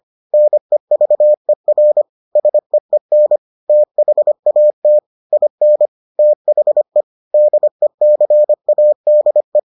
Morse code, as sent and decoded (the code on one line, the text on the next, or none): -. ...- . .-. / ... . -. / - .... .- - / .. -. / - .... / -.. -.-. .- -.. .
NEVER SEEN THAT IN THE DECADE